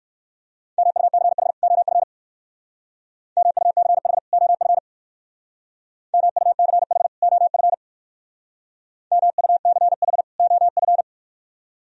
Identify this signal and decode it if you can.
Morse code, heard as a keyed tone, about 59 wpm, 700 Hz. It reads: MUCH OF MUCH OF MUCH OF MUCH OF